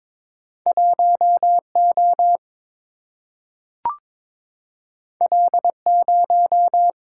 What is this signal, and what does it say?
Morse code, heard as a keyed tone, about 22 wpm, 700 Hz. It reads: 1 O L 0
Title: Single-Letter Single-Number 22wpm